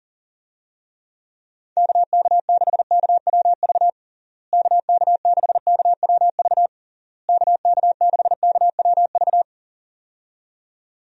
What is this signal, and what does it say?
Morse code, heard as a keyed tone, about 40 wpm, 700 Hz. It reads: KK6KWV KK6KWV KK6KWV
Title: USA Call Signs - 6 Characters - Repeated 3x - 40wpm